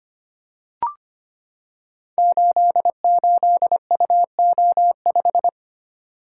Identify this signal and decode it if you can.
Morse code, heard as a keyed tone, about 25 wpm, 700 Hz. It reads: 88UO5
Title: Code Groups - 5 Letter:Numbers - 25wpm